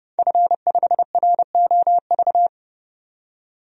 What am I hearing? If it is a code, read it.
Morse code, heard as a keyed tone, about 31 wpm, 700 Hz. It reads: F5ROV
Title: DXCC-100 Call Signs - 5 Characters - 30wpm